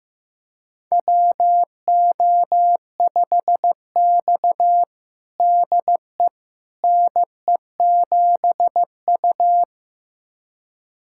Morse code, heard as a keyed tone, about 15 wpm, 700 Hz. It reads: WO5X DE NE7U
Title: USA Callsign DE Callsign - 4 Characters - 15wpm